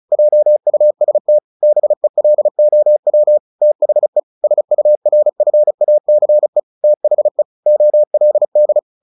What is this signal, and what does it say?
Morse code, heard as a keyed tone, about 34 wpm, 600 Hz. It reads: JUST BELOW THE SURFACE THE OLD
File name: the old man and the sea 2 35wpm